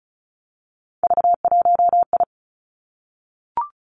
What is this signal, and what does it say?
Morse code, heard as a keyed tone, about 36 wpm, 700 Hz. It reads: V1I
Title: Three Challenging Letter-Number Combinations 35wpm